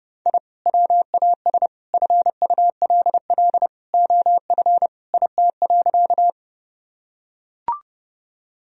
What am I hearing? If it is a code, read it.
Morse code, heard as a keyed tone, about 30 wpm, 700 Hz. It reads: I WAS FULL OF IT.
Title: Sentences from Top 1000 Words - Review Part 3 - 30wpm